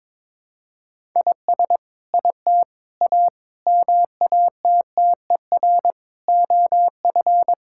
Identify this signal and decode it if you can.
Morse code, heard as a keyed tone, about 21 wpm, 700 Hz. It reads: IS IT A MATTER OF
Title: Sentences from Top 1000 Words - Review Part 5 - 22wpm